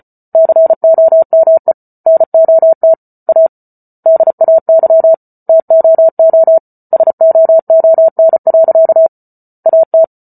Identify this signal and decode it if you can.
Morse code, heard as a keyed tone, about 34 wpm, 650 Hz. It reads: COME NOT A DAY TOO SOON. AT